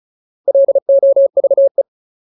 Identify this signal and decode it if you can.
Morse code, heard as a keyed tone, about 34 wpm, 550 Hz. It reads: LOVE
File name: Ultimate Word List / 35WPM Ultimate